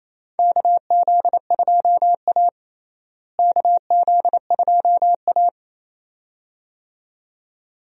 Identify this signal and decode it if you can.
Morse code, heard as a keyed tone, about 28 wpm, 700 Hz. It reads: KZ2A KZ2A
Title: USA Call Signs - 4 Characters - Repeated 2x - 28wpm